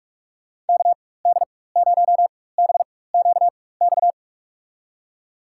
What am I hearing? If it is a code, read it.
Morse code, heard as a keyed tone, about 46 wpm, spaced as for 21 wpm, 700 Hz. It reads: KD0BQX
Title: USA Call Signs - 6 Characters - 4x Character Spacing - 45wpm